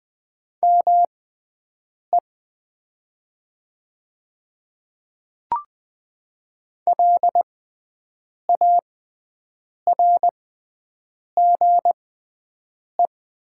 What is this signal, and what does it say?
Morse code, heard as a keyed tone, about 20 wpm, spaced as for 7 wpm, 700 Hz. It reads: ME LARGE
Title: 6x Character Spacing - Top 300 Words - 20wpm